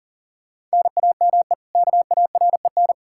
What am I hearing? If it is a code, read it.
Morse code, heard as a keyed tone, about 40 wpm, 700 Hz. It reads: NAME KAREN
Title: Practice QSO - Exchange 1 - Beginning - 40wpm